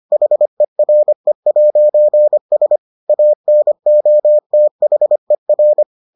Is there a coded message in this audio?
Yes: Morse code from 0.1 to 5.8 s, heard as a keyed tone, about 25 wpm, 600 Hz. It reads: HERE'S ANOTHER